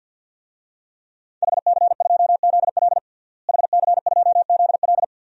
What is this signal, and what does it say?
Morse code, heard as a keyed tone, about 52 wpm, 700 Hz. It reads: HC2ZL HC2ZL
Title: DXCC-100 Call Signs - 5 Characters - Repeated 2x - 50wpm